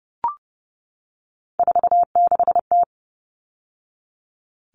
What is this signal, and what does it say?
Morse code, heard as a keyed tone, about 30 wpm, 700 Hz. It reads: 46T
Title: All Cut Numbers - 3-Digit 30wpm